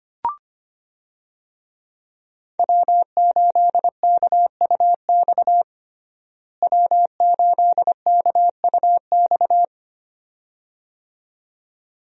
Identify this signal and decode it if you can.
Morse code, heard as a keyed tone, about 25 wpm, 700 Hz. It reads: W8KUX W8KUX
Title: USA Call Signs - 5 Characters - Repeated 2x - 25wpm